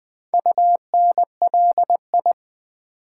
Morse code, heard as a keyed tone, about 20 wpm, 700 Hz. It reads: UNLI
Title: Build A Word - 2 Letters At A Time - 8-Letter Words - 20wpm